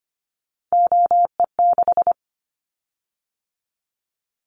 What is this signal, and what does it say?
Morse code, heard as a keyed tone, about 25 wpm, 700 Hz. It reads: OE6
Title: Callsign Prefixes for Top 50 DXCC - 25wpm